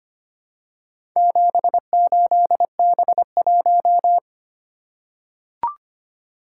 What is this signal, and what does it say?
Morse code, heard as a keyed tone, about 25 wpm, 700 Hz. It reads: 78B1
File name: practice / 25wpm / Four Letter-Number Combinations - 25wpm